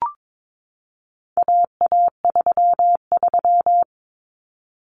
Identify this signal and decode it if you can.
Morse code, heard as a keyed tone, about 22 wpm, 700 Hz. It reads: AA33
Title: Common Cut Numbers - 4-Digit 22wpm